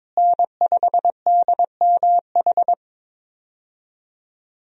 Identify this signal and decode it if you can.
Morse code, heard as a keyed tone, about 22 wpm, 700 Hz. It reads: N5DMH
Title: USA Call Signs - 5 Characters - Repeated 2x - 22wpm